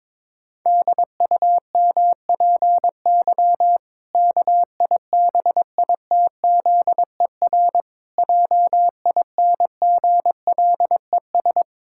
Morse code, heard as a keyed tone, about 22 wpm, 700 Hz. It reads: DUMPY KIBITZER JINGLES
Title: Pangram Challenge v2 - 22wpm